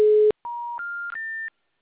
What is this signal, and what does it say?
Special information tone.